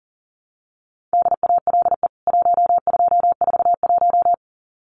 Morse code, heard as a keyed tone, about 37 wpm, 700 Hz. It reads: DALE 1241
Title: CWT Exchange 40wpm